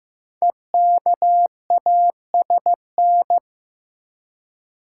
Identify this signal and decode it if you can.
Morse code, heard as a keyed tone, about 15 wpm, 700 Hz. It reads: EKASN
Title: Code Groups - 5 Letters - 15wpm